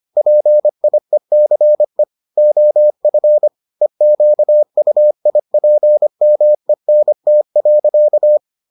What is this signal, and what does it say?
Morse code, heard as a keyed tone, about 25 wpm, 600 Hz. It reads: PIECE OF EQUIPMENT.